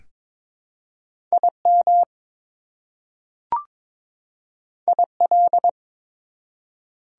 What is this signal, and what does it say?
Morse code, heard as a keyed tone, about 22 wpm, 700 Hz. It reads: IM IL
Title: Prefixes and Suffixes 22wpm